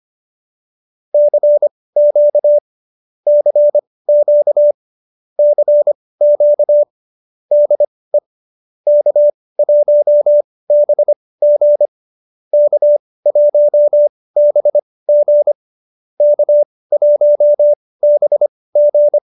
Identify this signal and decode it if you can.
Morse code, heard as a keyed tone, about 25 wpm, spaced as for 18 wpm, 590 Hz. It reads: CQ CQ CQ DE K1BG K1BG K1BG
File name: qso201b 18